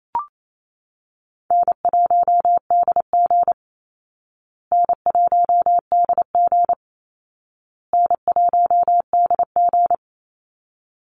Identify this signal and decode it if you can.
Morse code, heard as a keyed tone, about 28 wpm, 700 Hz. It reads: N1DG N1DG N1DG
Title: Call Signs Repeated 2x - 28wpm